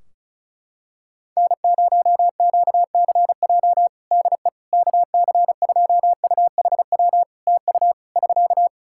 Morse code, heard as a keyed tone, about 36 wpm, 700 Hz. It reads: N0QCJ DE KC2UHW TU <SK>
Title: K1USN SST QSO Practice 35wpm